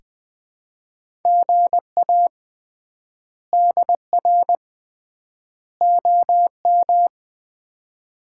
Morse code, heard as a keyed tone, about 20 wpm, 700 Hz. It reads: GA DR OM
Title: Common QSO Phrases - 3x Word-Spacing - 20wpm